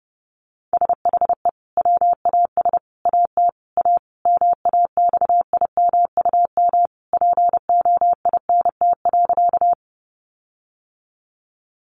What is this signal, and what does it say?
Morse code, heard as a keyed tone, about 30 wpm, 700 Hz. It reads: SHE WAS AT A MAXIMUM POINT.
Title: Sentences from Top 1300 Words 30wpm